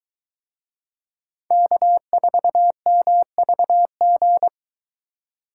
Morse code, heard as a keyed tone, about 22 wpm, 700 Hz. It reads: K4MVG